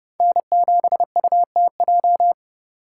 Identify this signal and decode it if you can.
Morse code, heard as a keyed tone, about 37 wpm, 700 Hz. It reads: N7UTJ
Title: Speed Racing - USA 1x3 Call Signs 30wpm